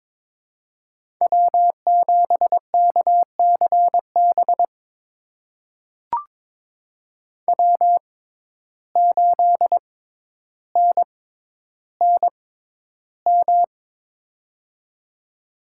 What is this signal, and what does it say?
Morse code, heard as a keyed tone, about 22 wpm, spaced as for 11 wpm, 700 Hz. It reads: W7KCB W8NNM
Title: USA Call Signs - 6x Character Spacing - 22wpm